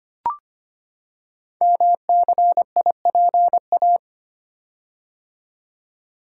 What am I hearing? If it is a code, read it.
Morse code, heard as a keyed tone, about 25 wpm, 700 Hz. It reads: MCIPA
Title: Code Groups - 5 Letters - 25wpm